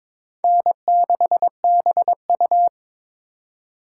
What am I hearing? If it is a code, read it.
Morse code, heard as a keyed tone, about 26 wpm, 700 Hz. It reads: N6BU
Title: Speed Racing - USA 1x2 Call Signs 22wpm